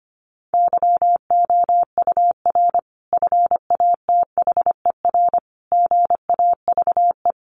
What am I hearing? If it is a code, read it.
Morse code, heard as a keyed tone, about 25 wpm, 700 Hz. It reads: YOUR FATHER GAVE
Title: Sentences from Top 1400 Words 25wpm